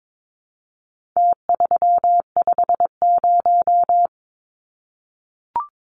Morse code, heard as a keyed tone, about 22 wpm, 700 Hz. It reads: T350
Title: Numbers - Contest Style - 4-Digit 22wpm